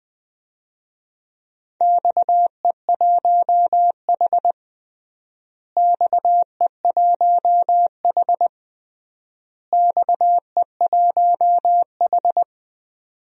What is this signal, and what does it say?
Morse code, heard as a keyed tone, about 19 wpm, 700 Hz. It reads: XE1H XE1H XE1H
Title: DXCC-100 Call Signs - 4 Characters - Repeated 3x - 20wpm